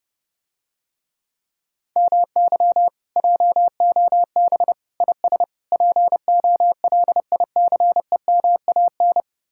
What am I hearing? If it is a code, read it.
Morse code, heard as a keyed tone, about 30 wpm, 700 Hz. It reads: MY JOB IS POLICEMAN